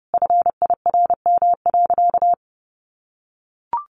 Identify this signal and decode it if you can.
Morse code, heard as a keyed tone, about 30 wpm, 700 Hz. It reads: FIRM.
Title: Sentences from Top 1400 Words 30wpm